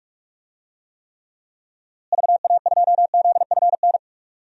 Morse code, heard as a keyed tone, about 46 wpm, 700 Hz. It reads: VA2ZFN
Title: DXCC-100 Call Signs - 6 Characters - Repeated 3x - 45wpm